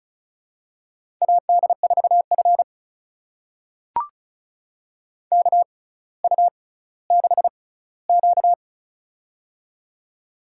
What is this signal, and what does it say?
Morse code, heard as a keyed tone, about 34 wpm, spaced as for 18 wpm, 700 Hz. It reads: AD4F KU6Q
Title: USA Call Signs - 4 Characters - 6x Character Spacing - 35wpm